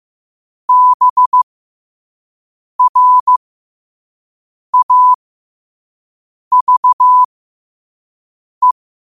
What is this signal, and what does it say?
Morse code, heard as a keyed tone, about 15 wpm, spaced as for 5 wpm, 1000 Hz. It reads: BRAVE
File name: qotdfiles / 5wpm